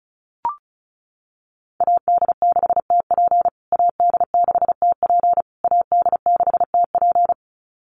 Morse code, heard as a keyed tone, about 34 wpm, 700 Hz. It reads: AD6TP AD6TP AD6TP
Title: USA Call Signs Repeated 3x 35wpm